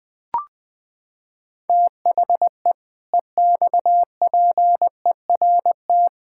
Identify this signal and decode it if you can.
Morse code, heard as a keyed tone, about 20 wpm, 700 Hz. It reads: THE EXPERT
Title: Sentences with 7-Word Phrases - 20wpm